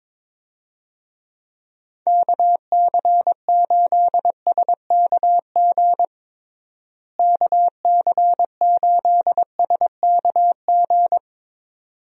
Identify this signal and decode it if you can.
Morse code, heard as a keyed tone, about 22 wpm, 700 Hz. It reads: KC8SKG KC8SKG
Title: USA Call Signs - 6 Characters - Repeated 2x - 22wpm